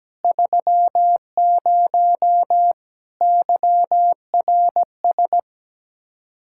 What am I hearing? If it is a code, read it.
Morse code, heard as a keyed tone, about 17 wpm, 700 Hz. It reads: 30 YRS
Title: Speed-Racing Common QSO Phrases - 17wpm